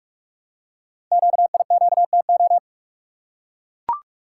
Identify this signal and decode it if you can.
Morse code, heard as a keyed tone, about 44 wpm, 700 Hz. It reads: QIQTO
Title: Code Groups - 5 Letters - 45wpm